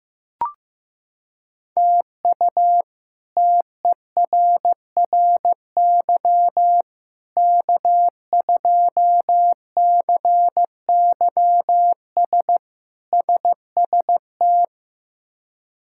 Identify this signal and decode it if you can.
Morse code, heard as a keyed tone, about 15 wpm, 700 Hz. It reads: TU TERRY K2CYS SST
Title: K1USN SST QSO Practice - 15wpm